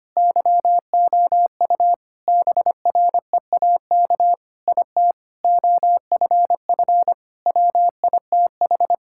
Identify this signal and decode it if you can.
Morse code, heard as a keyed tone, about 25 wpm, 700 Hz. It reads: YOU BREAK IT OFF WITH
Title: Sentences from Top 1000 Words - Review Part 6 - 25wpm